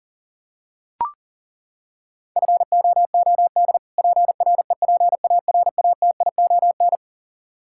Morse code, heard as a keyed tone, about 40 wpm, 700 Hz. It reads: FOOD PREPARATION
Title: K1USN SST QSO Practice 40wpm